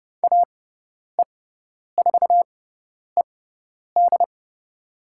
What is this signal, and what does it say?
Morse code, heard as a keyed tone, about 30 wpm, spaced as for 10 wpm, 700 Hz. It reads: AE4ED
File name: Lesson 18g - Letter m - Call Signs 30wpm